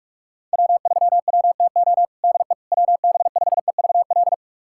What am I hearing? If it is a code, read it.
Morse code, heard as a keyed tone, about 46 wpm, 700 Hz. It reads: W3WTO DE WB5EVL
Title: USA Callsign DE Callsign 45wpm